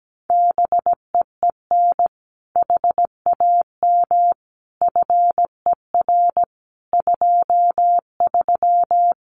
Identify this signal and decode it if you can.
Morse code, heard as a keyed tone, about 17 wpm, 700 Hz. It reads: BEEN HAM FER 23
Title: Common QSO Phrases Encore 17wpm